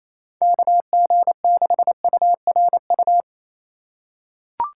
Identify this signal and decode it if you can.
Morse code, heard as a keyed tone, about 28 wpm, 700 Hz. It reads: KG6URU
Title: USA Call Signs - 4x Character Spacing - 28wpm